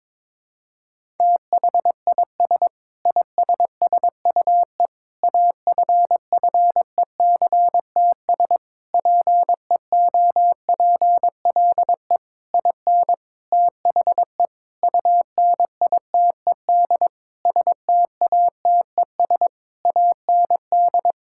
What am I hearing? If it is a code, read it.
Morse code, heard as a keyed tone, about 22 wpm, 700 Hz. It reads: THIS ISSUE AFFECTS PEOPLE IN THE UNITED STATES AND
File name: practice / 22wpm / Sentences with 8-Word Phrases - 22wpm